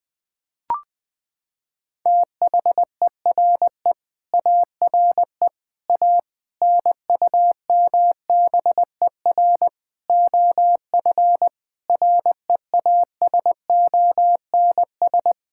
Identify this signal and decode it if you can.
Morse code, heard as a keyed tone, about 21 wpm, 700 Hz. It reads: THERE ARE A NUMBER OF REASONS
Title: Sentences with 8-Word Phrases - 20wpm